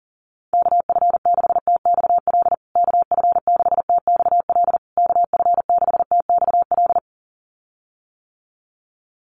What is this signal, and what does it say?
Morse code, heard as a keyed tone, about 40 wpm, 700 Hz. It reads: KF6TXL KF6TXL KF6TXL
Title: USA Call Signs Repeated 3x 40wpm